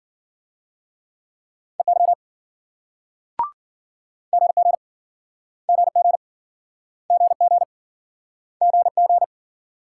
Morse code, heard as a keyed tone, about 57 wpm, 700 Hz. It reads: EY GG GG GG GG